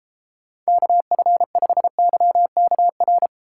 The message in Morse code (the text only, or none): KF5YKR